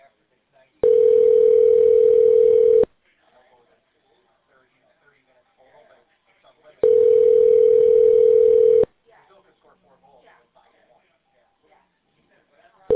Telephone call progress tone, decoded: ringback tone

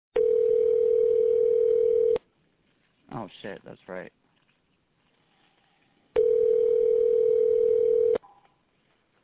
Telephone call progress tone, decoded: ringback tone

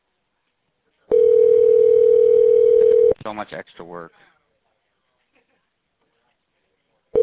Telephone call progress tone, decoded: ringback tone

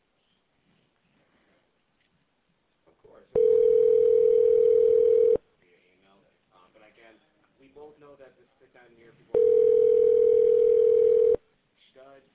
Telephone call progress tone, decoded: ringback tone

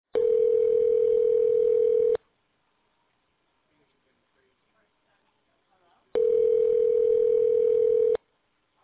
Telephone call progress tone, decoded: ringback tone

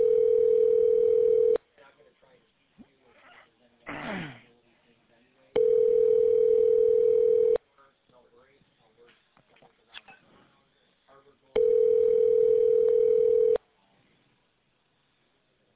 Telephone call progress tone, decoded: ringback tone

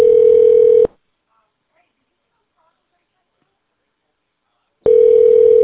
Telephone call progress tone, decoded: ringback tone

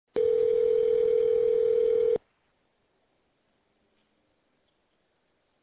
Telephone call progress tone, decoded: ringback tone

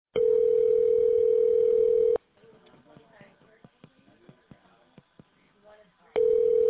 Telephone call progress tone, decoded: ringback tone

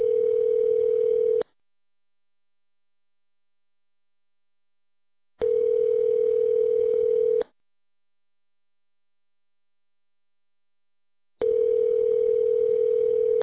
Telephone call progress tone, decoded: ringback tone

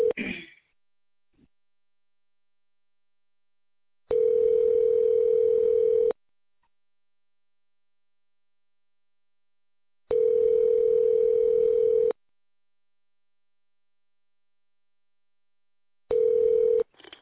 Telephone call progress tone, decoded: ringback tone

